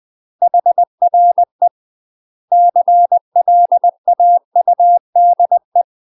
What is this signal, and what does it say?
Morse code, heard as a keyed tone, about 20 wpm, 700 Hz. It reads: HRE CLAUDE
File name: UK MASTER QSOs TRAINING / 20WPM / UK MASTER QSOs 0030